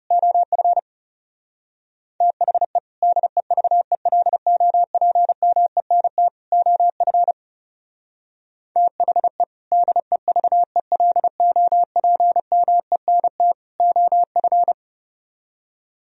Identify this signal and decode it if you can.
Morse code, heard as a keyed tone, about 32 wpm, 700 Hz. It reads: OF THE DEVELOPMENT OF THE DEVELOPMENT OF
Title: Speed-Racing - Sets of 3 Words 30wpm